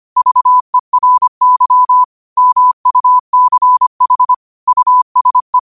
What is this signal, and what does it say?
Morse code, heard as a keyed tone, about 25 wpm, 1000 Hz. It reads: VERY MUCH USE
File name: qotdfiles / 25wpm